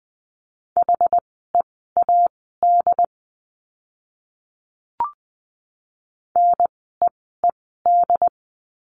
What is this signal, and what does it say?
Morse code, heard as a keyed tone, about 20 wpm, spaced as for 15 wpm, 700 Hz. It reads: HEAD NEED
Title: 101 Lesson 17f - Letter u - Words 20wpm